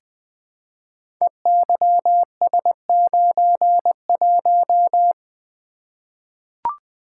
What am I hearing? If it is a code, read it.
Morse code, heard as a keyed tone, about 20 wpm, 700 Hz. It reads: EYS91
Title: Code Groups - 5 Letter:Numbers - 20wpm